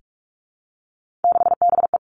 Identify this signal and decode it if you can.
Morse code, heard as a keyed tone, about 46 wpm, 700 Hz. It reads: 6BE
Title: Three Challenging Letter-Number Combinations 45wpm